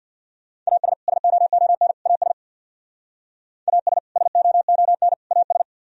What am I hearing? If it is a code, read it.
Morse code, heard as a keyed tone, about 63 wpm, 700 Hz. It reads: AS SOON AS AS SOON AS